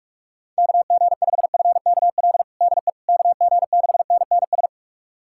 Morse code, heard as a keyed tone, about 45 wpm, 700 Hz. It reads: KG5FKL DE KG6NNS
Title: USA Callsign DE Callsign 45wpm